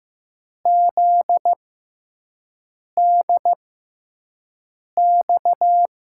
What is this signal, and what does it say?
Morse code, heard as a keyed tone, about 15 wpm, spaced as for 5 wpm, 700 Hz. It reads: ZDX
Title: USA Call Signs - 6 Characters - 6x Character Spacing - 15wpm